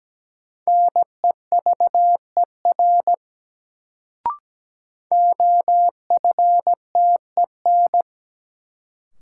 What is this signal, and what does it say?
Morse code, heard as a keyed tone, about 17 wpm, 700 Hz. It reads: NEVER OFTEN
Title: Sentences from Top 300 Words - New Word Drill - 17wpm